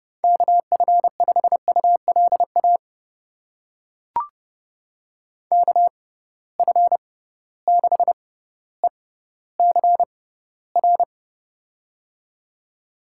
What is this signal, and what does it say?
Morse code, heard as a keyed tone, about 31 wpm, spaced as for 16 wpm, 700 Hz. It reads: KF5ULA KF6ECR
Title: USA Call Signs - 6 Characters - 6x Character Spacing - 30wpm